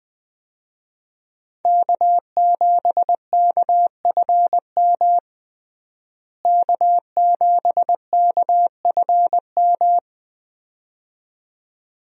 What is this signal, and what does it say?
Morse code, heard as a keyed tone, about 20 wpm, 700 Hz. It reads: K7KFM K7KFM
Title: USA Call Signs - 5 Characters - Repeated 2x - 20wpm